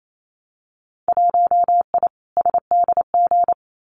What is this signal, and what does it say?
Morse code, heard as a keyed tone, about 28 wpm, 700 Hz. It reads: 1I SDG
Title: Winter Field Day Exchanges 28wpm